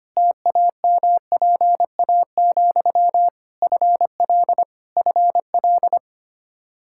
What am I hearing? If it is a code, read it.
Morse code, heard as a keyed tone, about 25 wpm, 700 Hz. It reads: TAMPA, FL FL
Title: K1USN SST QSO Practice 25wpm